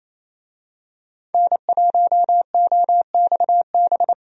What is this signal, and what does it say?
Morse code, heard as a keyed tone, about 27 wpm, 700 Hz. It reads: N1OXB
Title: USA Call Signs - 5 Characters - 28wpm